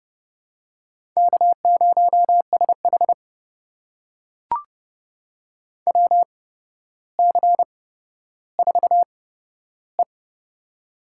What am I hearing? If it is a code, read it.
Morse code, heard as a keyed tone, about 30 wpm, spaced as for 13 wpm, 700 Hz. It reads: K0SH WC4E
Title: USA Call Signs - 4 Characters - 8x Character Spacing - 30wpm